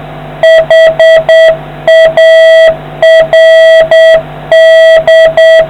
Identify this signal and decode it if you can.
Morse code, heard as a keyed tone, about 8 wpm, 650 Hz, with noise. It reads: HARD